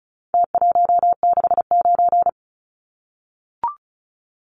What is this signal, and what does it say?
Morse code, heard as a keyed tone, about 34 wpm, 700 Hz. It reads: T169
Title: Numbers - Contest Style - 4-Digit 35wpm